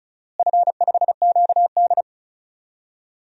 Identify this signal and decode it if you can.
Morse code, heard as a keyed tone, about 36 wpm, 700 Hz. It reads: F5QD
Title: DXCC-100 Call Signs - 4 Characters - Repeated 2x - 35wpm